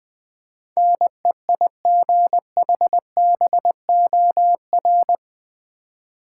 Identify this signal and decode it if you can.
Morse code, heard as a keyed tone, about 20 wpm, 700 Hz. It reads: NEIGHBOR
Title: Build A Word - 4 Letters At A Time - 11-Letter Words - 20wpm